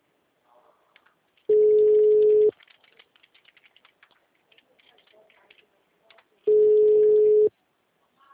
Ringback tone (Europe, 425 Hz).